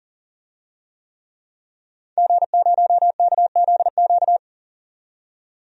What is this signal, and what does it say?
Morse code, heard as a keyed tone, about 40 wpm, 700 Hz. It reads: G0KZQ
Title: DXCC-100 Call Signs - 5 Characters - 40wpm